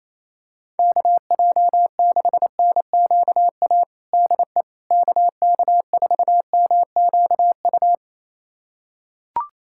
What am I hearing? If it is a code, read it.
Morse code, heard as a keyed tone, about 28 wpm, 700 Hz. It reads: KJ6NQA DE KK4MQU
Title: USA Callsign DE Callsign 28wpm